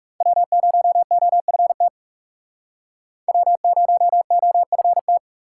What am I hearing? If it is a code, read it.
Morse code, heard as a keyed tone, about 39 wpm, 700 Hz. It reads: W0OFT W0OFT